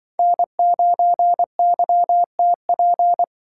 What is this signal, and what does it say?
Morse code, heard as a keyed tone, about 25 wpm, 700 Hz. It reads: N9YTP
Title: Speed Racing - USA 1x3 Call Signs 20wpm